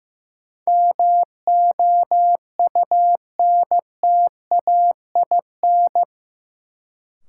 Morse code, heard as a keyed tone, about 15 wpm, 700 Hz. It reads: MOUNTAIN
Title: Sentences from Top 1000 Words - Review Part 1 - New Word Drill - 15wpm